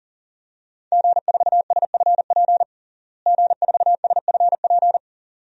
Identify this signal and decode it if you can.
Morse code, heard as a keyed tone, about 37 wpm, 700 Hz. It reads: G4SFP G4SFP
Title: DXCC-100 Call Signs - 5 Characters - Repeated 2x - 40wpm